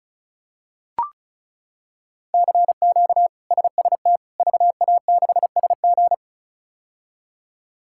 Morse code, heard as a keyed tone, about 34 wpm, 700 Hz. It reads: CQ SST VA6SG
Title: K1USN SST QSO Practice - 35wpm